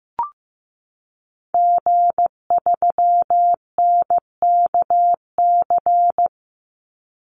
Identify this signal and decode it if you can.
Morse code, heard as a keyed tone, about 15 wpm, 700 Hz. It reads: G3NKC
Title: Call Signs Repeated 2x - 15wpm